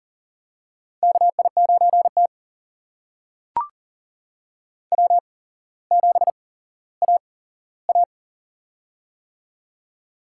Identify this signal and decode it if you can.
Morse code, heard as a keyed tone, about 42 wpm, spaced as for 17 wpm, 700 Hz. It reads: KI9T W7AA
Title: USA Call Signs - 4 Characters - 8x Character Spacing - 40wpm